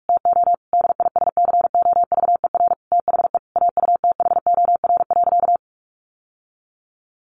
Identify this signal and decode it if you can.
Morse code, heard as a keyed tone, about 44 wpm, 700 Hz. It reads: TO DISCOVER THE AUTHOR.
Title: Sentences from Top 1200 Words 45wpm